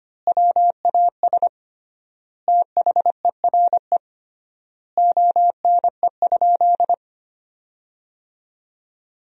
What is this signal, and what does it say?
Morse code, heard as a keyed tone, about 25 wpm, 700 Hz. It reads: WAS THERE ONE?
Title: 3x Word-Spacing - Sentences from Top 100 Words - 25wpm